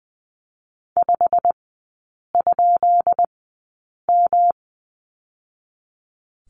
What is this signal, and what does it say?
Morse code, heard as a keyed tone, about 20 wpm, spaced as for 8 wpm, 700 Hz. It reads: 5?M
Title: 218 Lesson 35e - Number 0 - Three Characters 20wpm